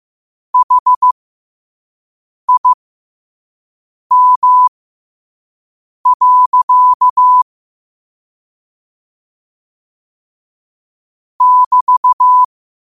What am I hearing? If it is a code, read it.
Morse code, heard as a keyed tone, about 15 wpm, spaced as for 5 wpm, 1000 Hz. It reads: HIM. <BT>